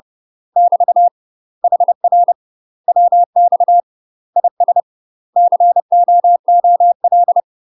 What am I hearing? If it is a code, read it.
Morse code, heard as a keyed tone, about 30 wpm, 700 Hz. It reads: <BT> HR WX IS COOL
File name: UK MASTER QSOs TRAINING / 30WPM / UK MASTER QSOs 0046